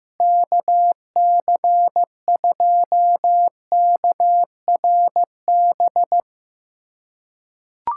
Morse code, heard as a keyed tone, about 15 wpm, 700 Hz. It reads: KC2KRB
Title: USA Callsign DE Callsign - 6 Characters - 15wpm